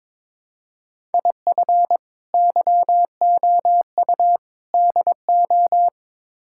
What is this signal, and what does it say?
Morse code, heard as a keyed tone, about 22 wpm, 700 Hz. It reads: IF YOU DO
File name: practice / 22wpm / Build A Phrase - 5-Word Phrases - 22wpm